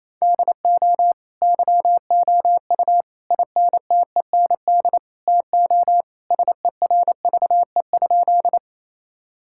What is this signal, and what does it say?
Morse code, heard as a keyed tone, about 28 wpm, 700 Hz. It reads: DO YOU INTEND TO SERVE?
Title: Sentences from Top 1600 Words 28wpm